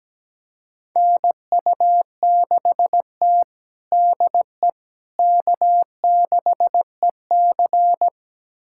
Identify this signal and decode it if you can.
Morse code, heard as a keyed tone, about 17 wpm, 700 Hz. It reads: NU6T DE K6EC
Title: USA Callsign DE Callsign - 4 Characters - 17wpm